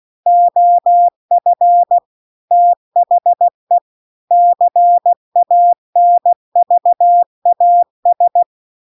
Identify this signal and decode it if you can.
Morse code, heard as a keyed tone, about 16 wpm, 700 Hz. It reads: OF THE CANVAS